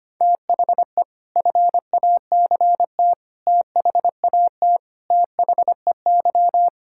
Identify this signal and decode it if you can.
Morse code, heard as a keyed tone, about 25 wpm, 700 Hz. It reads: THE FACT THAT THEY
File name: practice / 25wpm / Sentences with 4-Word Phrases - 25wpm